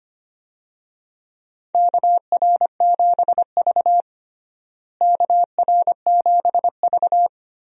Morse code, heard as a keyed tone, about 25 wpm, 700 Hz. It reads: KR7V KR7V
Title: USA Call Signs - 4 Characters - Repeated 2x - 25wpm